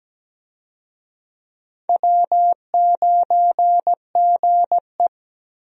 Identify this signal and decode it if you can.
Morse code, heard as a keyed tone, about 17 wpm, 700 Hz. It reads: W9GE
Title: USA Call Signs - 4 Characters - Repeated 2x - 17wpm